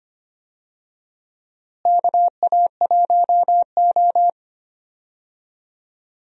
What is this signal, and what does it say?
Morse code, heard as a keyed tone, about 25 wpm, 700 Hz. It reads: KA1O
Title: USA Call Signs - 4 Characters - Repeated 3x - 25wpm